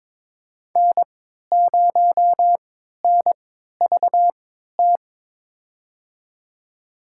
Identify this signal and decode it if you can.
Morse code, heard as a keyed tone, about 22 wpm, spaced as for 13 wpm, 700 Hz. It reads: N0NVT
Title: USA Call Signs - 5 Characters - 3x Character Spacing - 22wpm